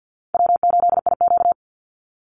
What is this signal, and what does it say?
Morse code, heard as a keyed tone, about 48 wpm, 700 Hz. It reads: W8IQ
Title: Easy Call Signs - 50wpm